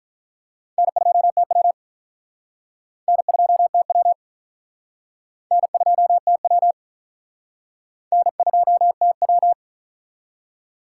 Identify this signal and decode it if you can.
Morse code, heard as a keyed tone, about 44 wpm, 700 Hz. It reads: N2TW N2TW N2TW N2TW